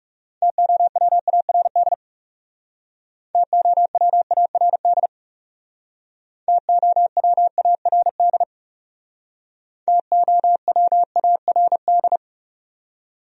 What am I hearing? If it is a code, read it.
Morse code, heard as a keyed tone, about 38 wpm, 700 Hz. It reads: TOWARD TOWARD TOWARD TOWARD
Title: Speed-Racing Top 500 Words 30wpm